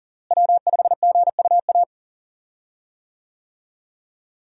Morse code, heard as a keyed tone, about 37 wpm, 700 Hz. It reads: W5GUA
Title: USA Call Signs - 5 Characters - Repeated 2x - 40wpm